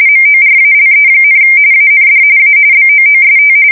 RTTY, about 45.45 baud. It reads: MORALE, AND A BOOST
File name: fsk170 / USS-New-Jersey